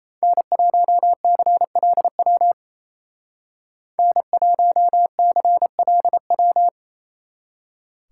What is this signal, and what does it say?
Morse code, heard as a keyed tone, about 33 wpm, 700 Hz. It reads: N1CLW N1CLW
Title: Speed Racing - USA Call Signs 28wpm